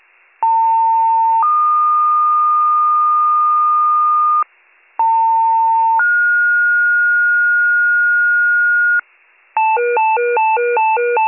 Two-tone page, heard as a one-second tone, then a three-second tone, about 900 then 1220 Hz, about 900 then 1430 Hz, with noise. It is heard from 0.4 to 9.0 s.